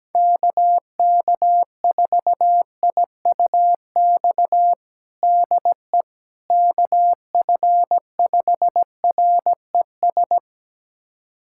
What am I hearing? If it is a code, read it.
Morse code, heard as a keyed tone, about 17 wpm, 700 Hz. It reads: KK4IUX DE KF5RES
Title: USA Callsign DE Callsign 17wpm